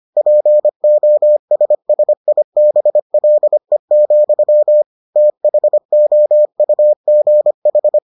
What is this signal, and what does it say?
Morse code, heard as a keyed tone, about 25 wpm, 600 Hz. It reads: POSSIBLE, THOUGH